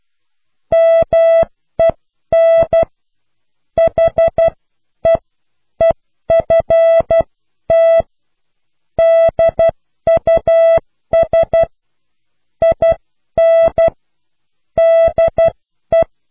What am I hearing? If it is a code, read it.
Morse code, heard as a keyed tone, about 12 wpm, spaced as for 10 wpm, 660 Hz. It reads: MEN HEEFT DUS IN DE